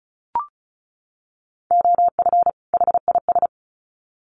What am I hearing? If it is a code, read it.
Morse code, heard as a keyed tone, about 34 wpm, 700 Hz. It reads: OF HIS
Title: Sets of 2 Words 35wpm